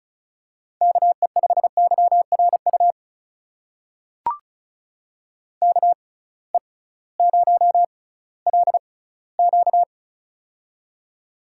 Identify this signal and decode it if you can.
Morse code, heard as a keyed tone, about 36 wpm, spaced as for 19 wpm, 700 Hz. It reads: KE5YRU KE0LQ
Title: USA Call Signs - 6x Character Spacing - 35wpm